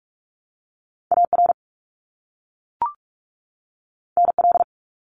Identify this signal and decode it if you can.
Morse code, heard as a keyed tone, about 44 wpm, 700 Hz. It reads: AR NL
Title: ARRL Sections 45wpm